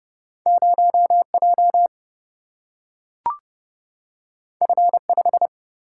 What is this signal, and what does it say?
Morse code, heard as a keyed tone, about 30 wpm, 700 Hz. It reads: 0 J F 5
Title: Single-Letter Single-Number 30wpm